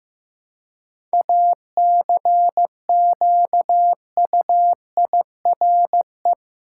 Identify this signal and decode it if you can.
Morse code, heard as a keyed tone, about 16 wpm, 700 Hz. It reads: ACQUIRE
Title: Seven Letter Words - 15wpm